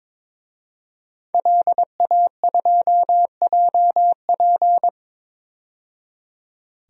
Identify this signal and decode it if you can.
Morse code, heard as a keyed tone, about 22 wpm, 700 Hz. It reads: LA2JP
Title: DXCC-100 Call Signs - 5 Characters - 22wpm